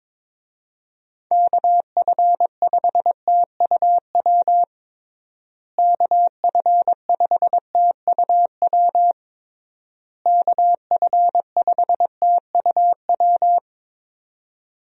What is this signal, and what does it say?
Morse code, heard as a keyed tone, about 22 wpm, 700 Hz. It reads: KF5TUW KF5TUW KF5TUW
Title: USA Call Signs - 6 Characters - Repeated 3x - 22wpm